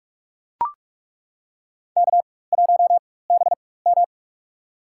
Morse code, heard as a keyed tone, about 42 wpm, spaced as for 20 wpm, 700 Hz. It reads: K1BM
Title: USA Call Signs - 4 Characters - 4x Character Spacing - 45wpm